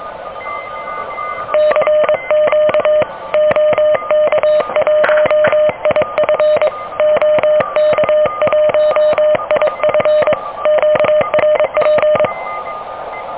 Morse code, heard as a keyed tone, about 24 wpm, 610 Hz, with radio noise. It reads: CQ OK1IF OK1IF QRP
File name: OK QRP 2023 / OK QRP 2023